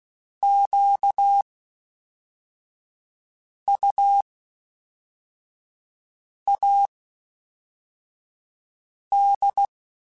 Morse code, heard as a keyed tone, about 16 wpm, spaced as for 4 wpm, 800 Hz. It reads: QUAD
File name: (spaced) / FromSMs spaced mp3s / Antennas sp@5